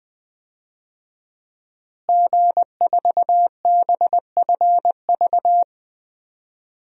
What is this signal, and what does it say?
Morse code, heard as a keyed tone, about 20 wpm, 700 Hz. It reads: G4BFV
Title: DXCC-100 Call Signs - 5 Characters - Repeated 2x - 20wpm